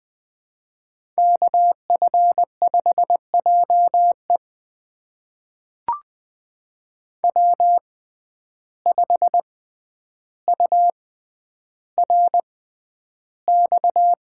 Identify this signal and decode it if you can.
Morse code, heard as a keyed tone, about 20 wpm, spaced as for 10 wpm, 700 Hz. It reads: KF5JE W5URX
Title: USA Call Signs - 5 Characters - 6x Character Spacing - 20wpm